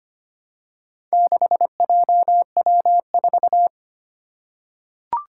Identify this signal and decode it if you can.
Morse code, heard as a keyed tone, about 25 wpm, 700 Hz. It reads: 6JW4
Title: Four Challenging Letter-Number Combinations 25wpm